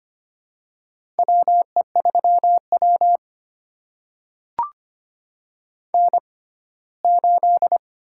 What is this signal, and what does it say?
Morse code, heard as a keyed tone, about 25 wpm, spaced as for 17 wpm, 700 Hz. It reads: WE3W N8
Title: USA Call Signs - 6x Character Spacing - 25wpm